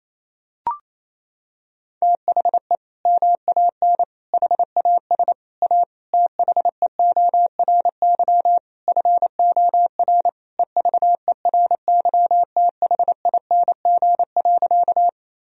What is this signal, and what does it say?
Morse code, heard as a keyed tone, about 28 wpm, 700 Hz. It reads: THE MAN HAS A THEORY FOR EVERYTHING.
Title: Sentences from Top 1000 Words - Review Part 3 - 28wpm